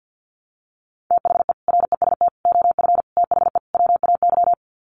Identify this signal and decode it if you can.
Morse code, heard as a keyed tone, about 50 wpm, 700 Hz. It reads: THE REST OF THE WAY
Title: Sets of 5 Words Encore 50wpm